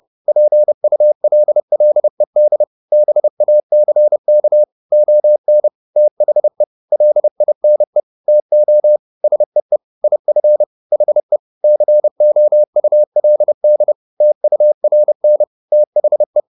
Morse code, heard as a keyed tone, about 30 wpm, 600 Hz. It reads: PULLED BACK ON THE LINE TO SEE IF HE COULD TURN THE
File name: the old man and the sea / the old man and the sea 1 30wpm